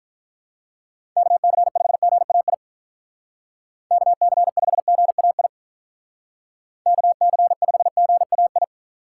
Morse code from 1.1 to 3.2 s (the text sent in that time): KC5GAI